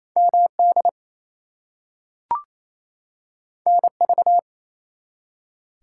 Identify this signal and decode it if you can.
Morse code, heard as a keyed tone, about 28 wpm, 700 Hz. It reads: MD NV
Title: US States and Canadian Provinces 28wpm